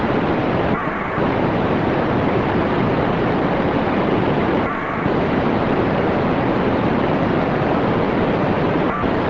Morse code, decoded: none